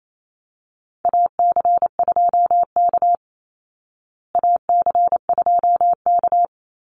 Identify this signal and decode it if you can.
Morse code, heard as a keyed tone, about 28 wpm, 700 Hz. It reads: AC2K AC2K
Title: Call Signs Repeated 3x - 28wpm